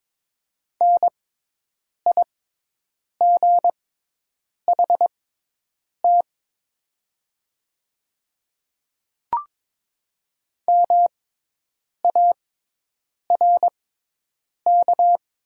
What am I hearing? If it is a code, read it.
Morse code, heard as a keyed tone, about 22 wpm, spaced as for 8 wpm, 700 Hz. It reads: NIGHT MARK
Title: 6x Character Spacing - Top 300 Words - 22wpm